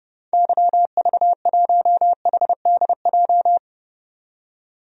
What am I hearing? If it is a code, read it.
Morse code, heard as a keyed tone, about 32 wpm, 700 Hz. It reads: YV1HDJ
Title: DXCC-100 Call Signs - 6 Characters - 30wpm